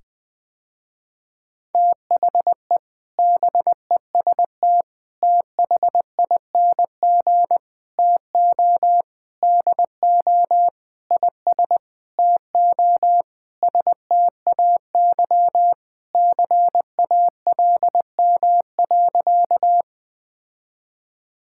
Morse code, heard as a keyed tone, about 20 wpm, 700 Hz. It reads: THE BEST THING TO DO IS TO STAY CALM.
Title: Sentences with 7-Word Phrases - Spoken Beginning - 20wpm